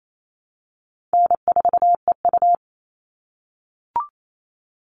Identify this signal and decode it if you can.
Morse code, heard as a keyed tone, about 28 wpm, 700 Hz. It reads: N4EU
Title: All Cut Numbers - 4-Digit 28wpm